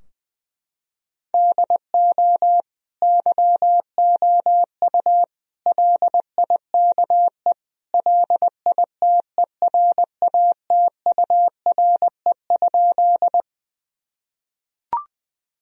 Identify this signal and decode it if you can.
Morse code, heard as a keyed tone, about 20 wpm, 700 Hz. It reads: DO YOU LIKE LITERATURE?
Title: Sentences from Top 1000 Words - Review Part 3 - 20wpm